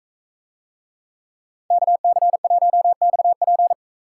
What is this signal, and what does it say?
Morse code, heard as a keyed tone, about 44 wpm, 700 Hz. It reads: KC1XP